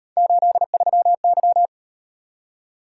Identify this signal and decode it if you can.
Morse code, heard as a keyed tone, about 38 wpm, 700 Hz. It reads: 83Y